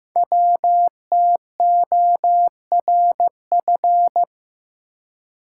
Morse code, heard as a keyed tone, about 15 wpm, 700 Hz. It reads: WTORF
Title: Code Groups - 5 Letter:Numbers - 15wpm